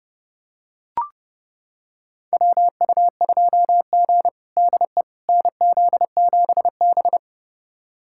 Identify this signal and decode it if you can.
Morse code, heard as a keyed tone, about 32 wpm, 700 Hz. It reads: WU2G DE NZ7B
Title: USA Callsign DE Callsign - 4 Characters - 30wpm